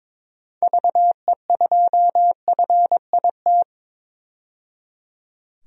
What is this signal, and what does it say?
Morse code, heard as a keyed tone, about 22 wpm, 700 Hz. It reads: VE2FIT
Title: DXCC-100 Call Signs - 6 Characters - 22wpm